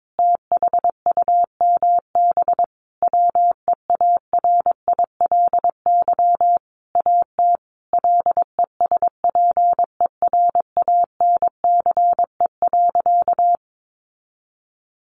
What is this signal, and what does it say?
Morse code, heard as a keyed tone, about 22 wpm, 700 Hz. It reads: THUMB WEARILY AT LESPERANCE.
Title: A Sound Of Thunder 22wpm